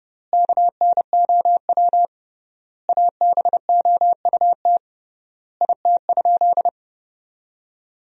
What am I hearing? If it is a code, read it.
Morse code, heard as a keyed tone, about 31 wpm, 700 Hz. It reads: KNOW ABOUT IT?
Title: 3x Word-Spacing - Sentences from Top 100 Words - 30wpm